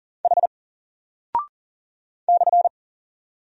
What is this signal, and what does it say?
Morse code, heard as a keyed tone, about 40 wpm, 700 Hz. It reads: H /
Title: Single Character (Letter:Number:Punctuation:Prosign) Alt 40wpm